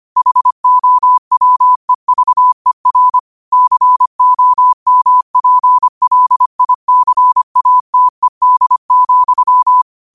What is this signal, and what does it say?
Morse code, heard as a keyed tone, about 25 wpm, 1000 Hz. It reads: HOWEVER COMPLICATED,